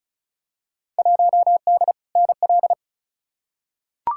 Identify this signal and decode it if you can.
Morse code, heard as a keyed tone, about 36 wpm, 700 Hz. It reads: 1D NL
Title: Field Day Exchanges Encore - 35wpm